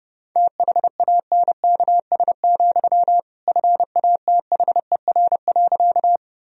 Morse code, heard as a keyed tone, about 30 wpm, 700 Hz. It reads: THANKS, FATHER.
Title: Sentences from Top 1000 Words - Review Part 3 - 30wpm